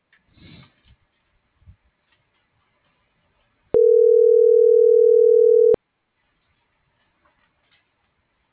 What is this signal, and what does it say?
Ringback tone (North America), 440 + 480 Hz.
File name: CENTRAL SOTOVA / out alejandro.rodriguez - .56689